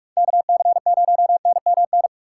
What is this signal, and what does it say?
Morse code, heard as a keyed tone, about 41 wpm, 700 Hz. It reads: KC0NMN
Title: USA Call Signs - 6 Characters - 2x Character Spacing - 45wpm